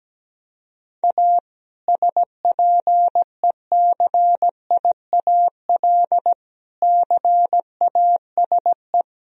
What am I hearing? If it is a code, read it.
Morse code, heard as a keyed tone, about 17 wpm, 700 Hz. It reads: A SPECIAL CASE
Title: Sentences from Top 1000 Words - Review Part 3 - 17wpm